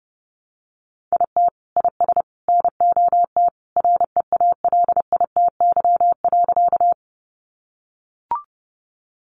Morse code, heard as a keyed tone, about 28 wpm, 700 Hz. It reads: IT IS NOT REALITY.
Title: Sentences from Top 1200 Words 30wpm